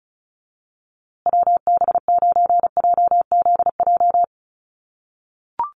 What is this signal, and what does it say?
Morse code, heard as a keyed tone, about 34 wpm, 700 Hz. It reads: WB9JZJ
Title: USA Call Signs Repeated 3x 35wpm